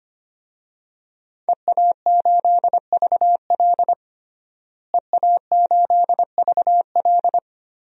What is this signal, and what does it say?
Morse code, heard as a keyed tone, about 25 wpm, 700 Hz. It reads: EA8VL EA8VL
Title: DXCC-100 Call Signs - 5 Characters - Repeated 3x - 25wpm